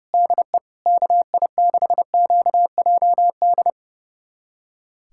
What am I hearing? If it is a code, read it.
Morse code, heard as a keyed tone, about 32 wpm, 700 Hz. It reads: DE KI6QJD
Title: USA Callsign DE Callsign - 6 Characters - 30wpm